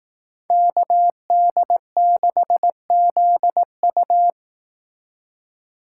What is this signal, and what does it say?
Morse code, heard as a keyed tone, about 18 wpm, 700 Hz. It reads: KD6ZU
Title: Speed Racing - USA 2x2 Call Signs 15wpm